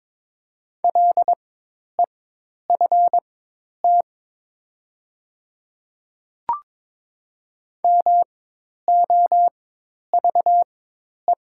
Morse code, heard as a keyed tone, about 22 wpm, spaced as for 10 wpm, 700 Hz. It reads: LEFT MOVE
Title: 4x Character Spacing - Top 500 Words - 22wpm